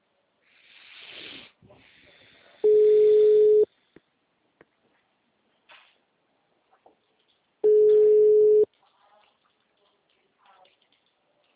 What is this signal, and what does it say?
Ringback tone (Europe, 425 Hz).
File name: out- -2017- - - .118